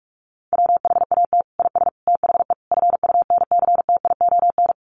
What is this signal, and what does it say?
Morse code, heard as a keyed tone, about 44 wpm, 700 Hz. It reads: WHAT IS THE FUNCTION
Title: Sentences from Top 1100 Words 45wpm